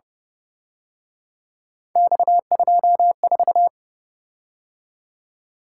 Morse code, heard as a keyed tone, about 29 wpm, 700 Hz. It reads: X24
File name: Speed-Racing - Three Letter-Number Combinations - 25wpm